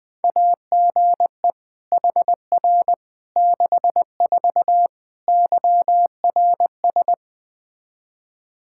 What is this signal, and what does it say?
Morse code, heard as a keyed tone, about 20 wpm, 700 Hz. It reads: AGE HR 64 YRS
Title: K1USN SST QSO Practice 20wpm